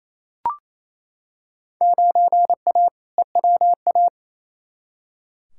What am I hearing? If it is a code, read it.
Morse code, heard as a keyed tone, about 28 wpm, 700 Hz. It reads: 9A EWA
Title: Field Day Exchanges Encore - 28wpm